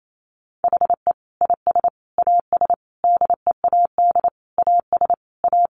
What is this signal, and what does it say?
Morse code, heard as a keyed tone, about 28 wpm, 700 Hz. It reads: HE IS AS DEAD AS A
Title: Sentences from Top 1500 Words 28wpm